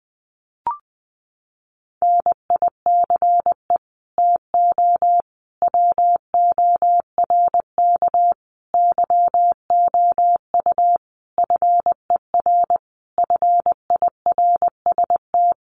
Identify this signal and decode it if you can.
Morse code, heard as a keyed tone, about 20 wpm, 700 Hz. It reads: NICE TO WORK YOU FER FIRST
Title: Common QSO Phrases 20wpm